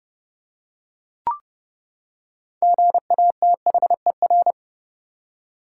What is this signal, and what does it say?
Morse code, heard as a keyed tone, about 31 wpm, 700 Hz. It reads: GATHER
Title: Six Letter Words - 30wpm